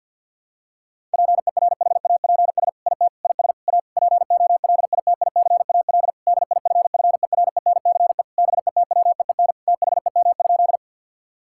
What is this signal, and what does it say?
Morse code, heard as a keyed tone, about 52 wpm, 700 Hz. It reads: PERHAPS IT IS A POLITICAL DIFFERENCE BETWEEN THEM?
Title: Sentences from Top 1000 Words - Review Part 5 - 50wpm